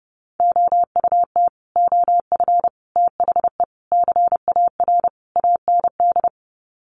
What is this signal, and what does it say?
Morse code, heard as a keyed tone, about 28 wpm, 700 Hz. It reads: OUT OF THE CAR AND
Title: Sets of 5 Words Encore 30wpm